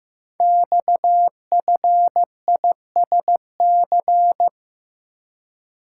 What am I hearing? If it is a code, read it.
Morse code, heard as a keyed tone, about 15 wpm, 700 Hz. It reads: XFISC
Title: Code Groups - 5 Letters - 15wpm